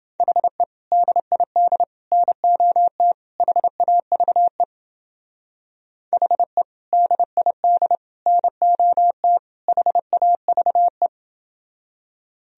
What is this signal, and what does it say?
Morse code, heard as a keyed tone, about 31 wpm, 700 Hz. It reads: HE DID NOT HAVE HE DID NOT HAVE